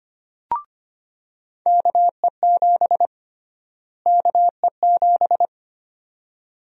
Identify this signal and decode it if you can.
Morse code, heard as a keyed tone, about 25 wpm, 700 Hz. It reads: KE7 KE7
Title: USA Call Sign Prefixes - Repeated 2x - 25wpm